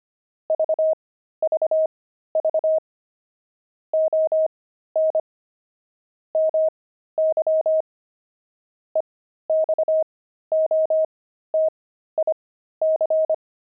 Morse code, heard as a keyed tone, about 25 wpm, spaced as for 13 wpm, 630 Hz. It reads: VVV ON MY EXOTIC
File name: pangram-fs-13-cs-25-journey